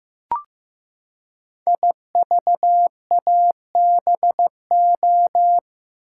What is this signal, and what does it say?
Morse code, heard as a keyed tone, about 15 wpm, 700 Hz. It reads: IVABO
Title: Code Groups - 5 Letters - 15wpm